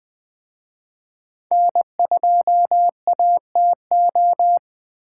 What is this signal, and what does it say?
Morse code, heard as a keyed tone, about 19 wpm, 700 Hz. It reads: N2ATO